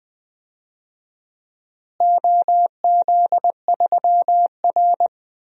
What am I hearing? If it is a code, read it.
Morse code, heard as a keyed tone, about 21 wpm, 700 Hz. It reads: OZ3R
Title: DXCC-100 Call Signs - 4 Characters - Repeated 2x - 20wpm